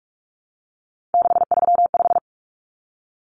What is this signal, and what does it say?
Morse code, heard as a keyed tone, about 44 wpm, 700 Hz. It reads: 635
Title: Common Cut Numbers - 3-Digit 45wpm